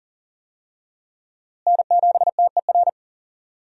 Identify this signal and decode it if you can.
Morse code, heard as a keyed tone, about 37 wpm, 700 Hz. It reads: N7TER
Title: USA Call Signs - 5 Characters - Repeated 3x - 40wpm